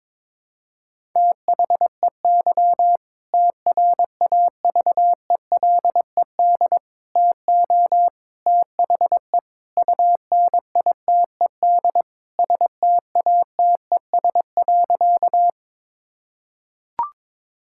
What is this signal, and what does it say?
Morse code, heard as a keyed tone, about 22 wpm, 700 Hz. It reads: THEY TRAVELED TO THE UNITED STATES.
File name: practice / 22wpm / Sentences with 2-Word Phrases - 22wpm